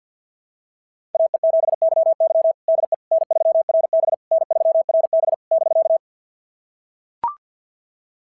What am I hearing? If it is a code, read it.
Morse code, heard as a keyed tone, about 46 wpm, 660 Hz. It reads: AE7YY DE N3RB N3RB <BK>
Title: SOTA QSO Practice 50wpm